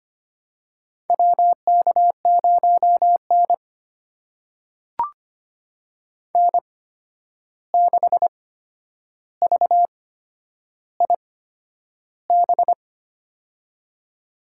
Morse code, heard as a keyed tone, about 25 wpm, spaced as for 10 wpm, 700 Hz. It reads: WK0N N6VIB
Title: USA Call Signs - 8x Character Spacing - 25wpm